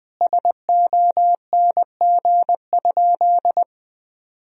Morse code, heard as a keyed tone, about 21 wpm, 700 Hz. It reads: SONG?
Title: Sentences from Top 1000 Words - Review Part 2 - 20wpm